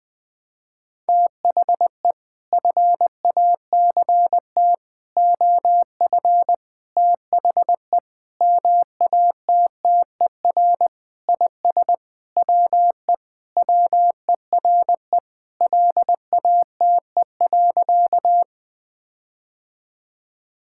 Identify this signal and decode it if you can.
Morse code, heard as a keyed tone, about 20 wpm, 700 Hz. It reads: THE FACT OF THE MATTER IS WE WERE LATE.
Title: Sentences with 5-Word Phrases - 20wpm